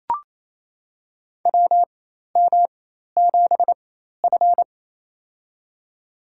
Morse code, heard as a keyed tone, about 28 wpm, spaced as for 13 wpm, 700 Hz. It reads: WM7F
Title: USA Call Signs - 4x Character Spacing - 28wpm